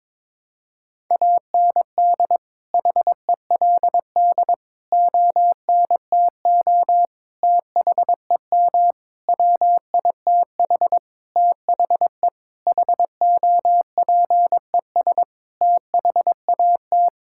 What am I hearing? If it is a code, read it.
Morse code, heard as a keyed tone, about 22 wpm, 700 Hz. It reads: AND HELD ONTO THEM WITH THE HOPES THAT
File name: Sentences with 9-Word Phrases - 22wpm